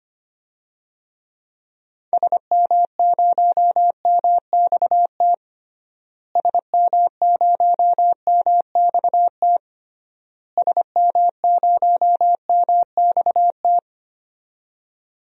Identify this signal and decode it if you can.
Morse code, heard as a keyed tone, about 25 wpm, 700 Hz. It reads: SM0MXT SM0MXT SM0MXT
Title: DXCC-100 Call Signs - 6 Characters - Repeated 3x - 25wpm